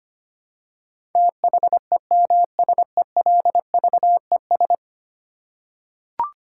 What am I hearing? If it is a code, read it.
Morse code, heard as a keyed tone, about 25 wpm, 700 Hz. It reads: THEMSELVES
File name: 10 Letter Words - First 3 Letters Given - 25wpm